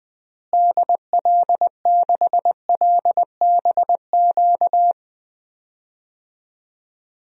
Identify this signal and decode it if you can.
Morse code, heard as a keyed tone, about 19 wpm, 700 Hz. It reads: DL6LBQ
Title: DXCC-100 Call Signs - 6 Characters - Repeated 3x - 20wpm